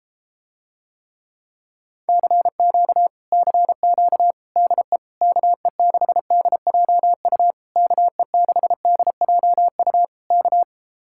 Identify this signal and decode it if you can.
Morse code, heard as a keyed tone, about 33 wpm, 700 Hz. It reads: CQ CQ DE KE6DJU KE6DJU K